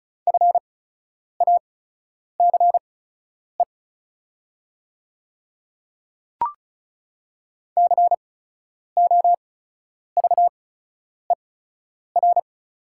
Morse code, heard as a keyed tone, about 36 wpm, spaced as for 10 wpm, 700 Hz. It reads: FACE COVER
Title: 8x Character Spacing - Top 500 Words - 35wpm